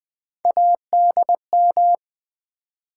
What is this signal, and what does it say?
Morse code, heard as a keyed tone, about 20 wpm, 700 Hz. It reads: ADM
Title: Build A Word - 3 Letters At A Time - 6-Letter Words - 20wpm